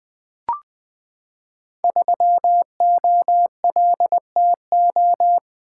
Morse code, heard as a keyed tone, about 20 wpm, 700 Hz. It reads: 3OLTO
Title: Code Groups - 5 Letter:Numbers - 20wpm